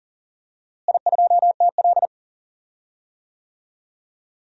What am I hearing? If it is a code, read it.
Morse code, heard as a keyed tone, about 40 wpm, 700 Hz. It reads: I2TL
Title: DXCC-100 Call Signs - 4 Characters - Repeated 3x - 40wpm